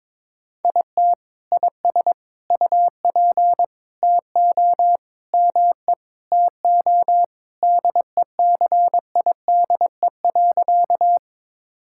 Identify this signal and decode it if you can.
Morse code, heard as a keyed tone, about 22 wpm, 700 Hz. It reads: IT IS UP TO ME TO DECIDE.
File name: practice / 22wpm / Sentences with 2-Word Phrases - 22wpm